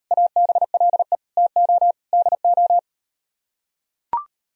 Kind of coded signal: Morse code

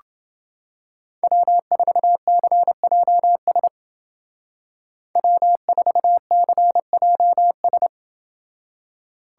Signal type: Morse code

